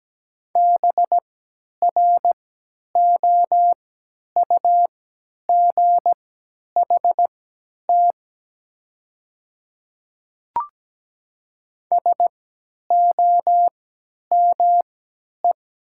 Morse code, heard as a keyed tone, about 17 wpm, spaced as for 10 wpm, 700 Hz. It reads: BROUGHT SOME
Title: 3x Character Spacing - Top 500 Words - 17wpm